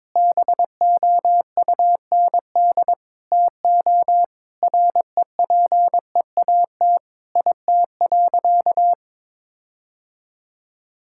Morse code, heard as a keyed tone, about 22 wpm, 700 Hz. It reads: BOUND TO REPEAT IT.
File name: practice / 22wpm / Sentences with 8-Word Phrases - 22wpm